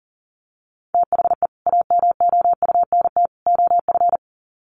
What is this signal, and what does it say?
Morse code, heard as a keyed tone, about 40 wpm, 700 Hz. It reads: THE AMOUNT OF
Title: Sets of 3 Words 40wpm